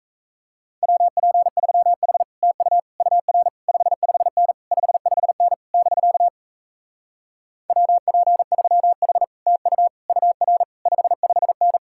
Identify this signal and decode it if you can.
Morse code, heard as a keyed tone, about 37 wpm, 700 Hz. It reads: WP3H TU UR 55N 55N <BK> WP3H TU UR 55N